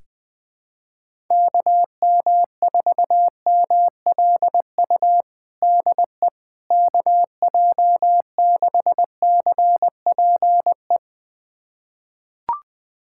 Morse code, heard as a keyed tone, about 20 wpm, 700 Hz. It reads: KM4MLU DE KJ6CPE
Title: USA Callsign DE Callsign 20wpm